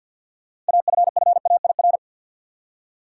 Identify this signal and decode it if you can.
Morse code, heard as a keyed tone, about 46 wpm, 700 Hz. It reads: AFFAIR
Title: Six Letter Words - 50wpm